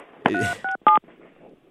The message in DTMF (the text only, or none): 367